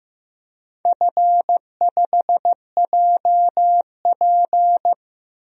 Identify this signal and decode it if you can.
Morse code, heard as a keyed tone, about 14 wpm, 700 Hz. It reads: F5JP
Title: DXCC-100 Call Signs - 4 Characters - Repeated 2x - 15wpm